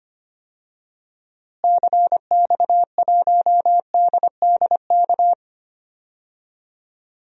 Morse code, heard as a keyed tone, about 25 wpm, 700 Hz. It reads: CX1DDK
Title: DXCC-100 Call Signs - 6 Characters - Repeated 3x - 25wpm